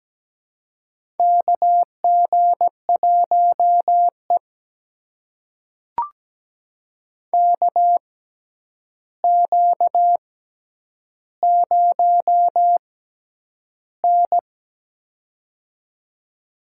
Morse code, heard as a keyed tone, about 17 wpm, spaced as for 9 wpm, 700 Hz. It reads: KG1E KQ0N
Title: USA Call Signs - 4 Characters - 6x Character Spacing - 17wpm